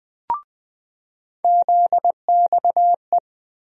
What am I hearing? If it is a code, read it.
Morse code, heard as a keyed tone, about 20 wpm, 700 Hz. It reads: ZXE
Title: USA Call Sign Postfixes - 3 Letters 20wpm